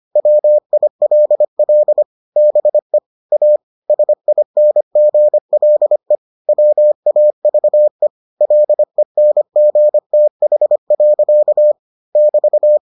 Morse code, heard as a keyed tone, about 25 wpm, 600 Hz. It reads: WILL BE A SINGLE WAVE LENGTH. <BT>